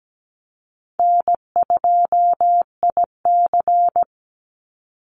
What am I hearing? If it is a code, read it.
Morse code, heard as a keyed tone, about 17 wpm, 700 Hz. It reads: N2IC
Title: Easy Call Signs Repeated 2x - 17wpm